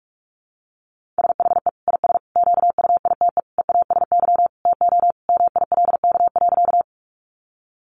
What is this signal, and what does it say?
Morse code, heard as a keyed tone, about 44 wpm, 700 Hz. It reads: SHE IS QUITE EASY TO MILK.
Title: Sentences from Top 1500 Words 45wpm